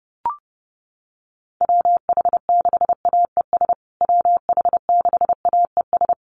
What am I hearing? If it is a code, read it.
Morse code, heard as a keyed tone, about 28 wpm, 700 Hz. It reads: WH6AES WH6AES
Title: USA Call Signs Repeated 3x 30wpm